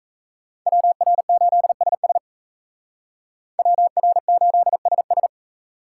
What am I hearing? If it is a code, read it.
Morse code, heard as a keyed tone, about 33 wpm, 700 Hz. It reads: WR8SS WR8SS
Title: Speed Racing - USA 2x2 Call Signs 28wpm